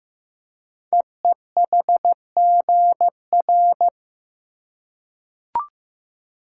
Morse code, heard as a keyed tone, about 15 wpm, 700 Hz. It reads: EEHGR
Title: Code Groups - 5 Letters - 15wpm